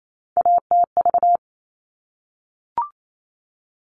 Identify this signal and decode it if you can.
Morse code, heard as a keyed tone, about 28 wpm, 700 Hz. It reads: ATV
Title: All Cut Numbers - 3-Digit 28wpm